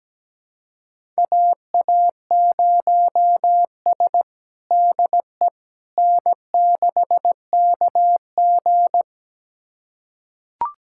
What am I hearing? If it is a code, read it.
Morse code, heard as a keyed tone, about 17 wpm, 700 Hz. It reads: AA0S DE N6KG
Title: USA Callsign DE Callsign - 4 Characters - 17wpm